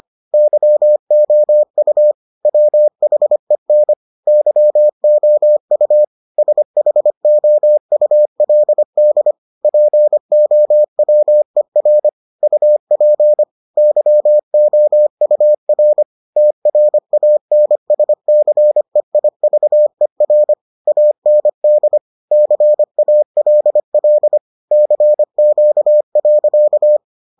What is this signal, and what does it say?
Morse code, heard as a keyed tone, about 25 wpm, 600 Hz. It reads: YOU WHEN YOU SHOULD POWER UP YOUR TRANSCEIVER AND CALL CQ.